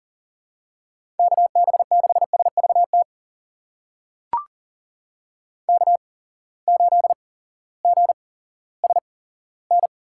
Morse code, heard as a keyed tone, about 37 wpm, spaced as for 18 wpm, 700 Hz. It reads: KB6SVT K8GSN
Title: USA Call Signs - 8x Character Spacing - 40wpm